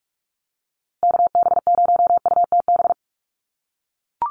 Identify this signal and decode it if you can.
Morse code, heard as a keyed tone, about 44 wpm, 700 Hz. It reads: KB0UTB
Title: USA Call Signs Repeated 3x 45wpm